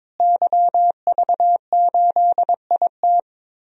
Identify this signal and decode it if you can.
Morse code, heard as a keyed tone, about 21 wpm, 700 Hz. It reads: YV8IT
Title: DXCC-100 Call Signs - 5 Characters - Repeated 3x - 22wpm